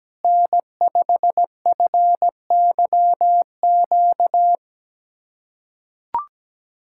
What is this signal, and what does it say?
Morse code, heard as a keyed tone, about 17 wpm, 700 Hz. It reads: N5FYQ
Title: USA Callsign DE Callsign - 5 Characters - 17wpm